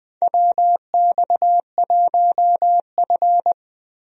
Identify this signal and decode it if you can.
Morse code, heard as a keyed tone, about 20 wpm, 700 Hz. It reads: WX1F
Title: USA Call Signs - 4 Characters - Repeated 3x - 20wpm